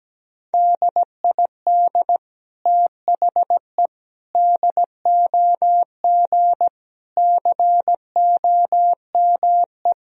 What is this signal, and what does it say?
Morse code, heard as a keyed tone, about 17 wpm, 700 Hz. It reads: DID THE DOG COME
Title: Sentences from Top 1000 Words - Review Part 2 - 17wpm